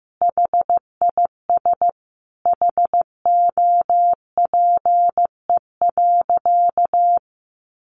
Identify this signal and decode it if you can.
Morse code, heard as a keyed tone, about 15 wpm, 700 Hz. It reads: HIS HOPE.
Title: Sentences from Top 1100 Words 15wpm